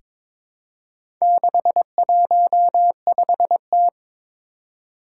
Morse code, heard as a keyed tone, about 22 wpm, 700 Hz. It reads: 615T
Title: Four Challenging Letter-Number Combinations 22wpm